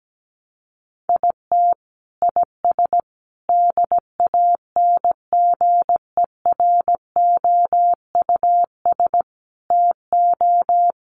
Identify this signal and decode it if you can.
Morse code, heard as a keyed tone, about 17 wpm, 700 Hz. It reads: IT IS DANGEROUS TO
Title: Sentences from Top 1200 Words 17wpm